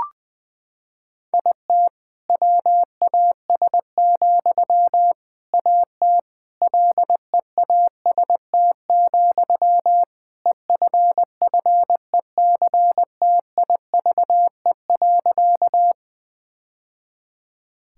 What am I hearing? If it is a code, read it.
Morse code, heard as a keyed tone, about 20 wpm, 700 Hz. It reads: IT WAS, AT LEAST, EFFECTIVE.
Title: Sentences from Top 1000 Words - Review Part 6 - 20wpm